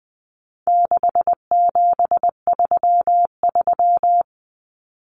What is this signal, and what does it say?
Morse code, heard as a keyed tone, about 20 wpm, 700 Hz. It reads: 6733
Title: Numbers - 4-Digit 20wpm